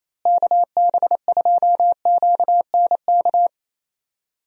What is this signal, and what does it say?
Morse code, heard as a keyed tone, about 28 wpm, 700 Hz. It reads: KB2QNK
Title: USA Call Signs - 6 Characters - Repeated 3x - 28wpm